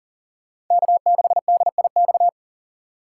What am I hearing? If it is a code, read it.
Morse code, heard as a keyed tone, about 40 wpm, 700 Hz. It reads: K6DIX
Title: USA Call Signs - 5 Characters - 2x Character Spacing - 40wpm